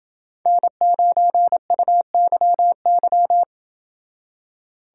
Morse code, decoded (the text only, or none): N9UYY